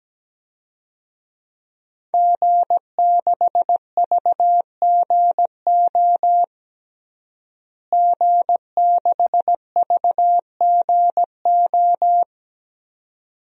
Morse code, heard as a keyed tone, about 17 wpm, 700 Hz. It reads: G6VGO G6VGO
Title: DXCC-100 Call Signs - 5 Characters - Repeated 2x - 17wpm